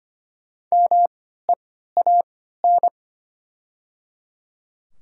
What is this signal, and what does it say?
Morse code, heard as a keyed tone, about 25 wpm, spaced as for 14 wpm, 700 Hz. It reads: MEAN
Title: 3x Character Spacing - Top 300 Words - 25wpm